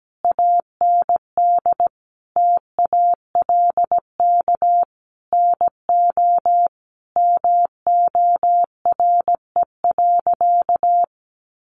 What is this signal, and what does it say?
Morse code, heard as a keyed tone, about 17 wpm, 700 Hz. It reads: AND TALK NO MORE.
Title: Sentences from Top 1400 Words 17wpm